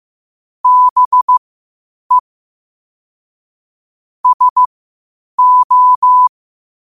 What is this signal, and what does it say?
Morse code, heard as a keyed tone, about 15 wpm, spaced as for 8 wpm, 1000 Hz. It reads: BE SO
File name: qotdfiles / 8wpm